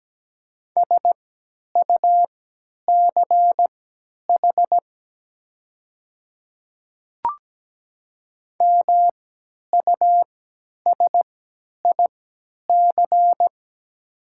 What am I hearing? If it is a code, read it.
Morse code, heard as a keyed tone, about 17 wpm, spaced as for 10 wpm, 700 Hz. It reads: SUCH MUSIC
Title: 3x Character Spacing - Top 300 Words - 17wpm